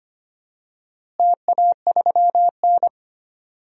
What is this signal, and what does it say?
Morse code, heard as a keyed tone, about 25 wpm, 700 Hz. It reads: TA3N
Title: DXCC-100 Call Signs - 4 Characters - Repeated 2x - 25wpm